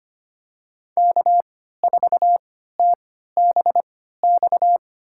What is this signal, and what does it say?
Morse code, heard as a keyed tone, about 25 wpm, spaced as for 14 wpm, 700 Hz. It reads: K4TBX
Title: USA Call Signs - 5 Characters - 3x Character Spacing - 25wpm